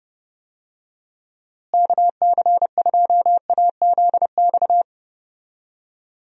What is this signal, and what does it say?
Morse code, heard as a keyed tone, about 32 wpm, 700 Hz. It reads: KC2AZX
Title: USA Call Signs - 6 Characters - Repeated 3x - 30wpm